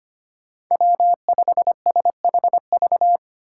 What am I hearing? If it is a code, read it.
Morse code, heard as a keyed tone, about 25 wpm, 700 Hz. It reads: W5SHV
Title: USA Call Signs - 2x Character Spacing - 25wpm